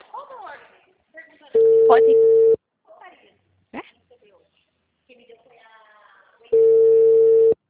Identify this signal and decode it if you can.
Ringback tone (China, 450 Hz), with speech.